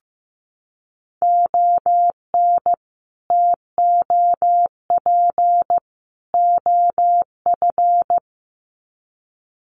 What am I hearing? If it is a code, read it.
Morse code, heard as a keyed tone, about 15 wpm, 700 Hz. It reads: ON TOP OF
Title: Sets of 3 Words 15wpm